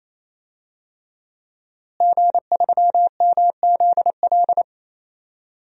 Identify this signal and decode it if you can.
Morse code, heard as a keyed tone, about 27 wpm, 700 Hz. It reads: G3MZL
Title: DXCC-100 Call Signs - 5 Characters - Repeated 3x - 28wpm